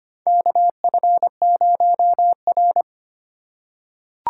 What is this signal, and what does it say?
Morse code, heard as a keyed tone, about 25 wpm, 700 Hz. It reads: KF0R
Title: USA Call Signs - 4 Characters - 3x Character Spacing - 25wpm